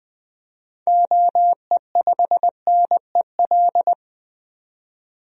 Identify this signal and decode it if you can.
Morse code, heard as a keyed tone, about 20 wpm, 700 Hz. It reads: OE5NEL
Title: DXCC-100 Call Signs - 6 Characters - 20wpm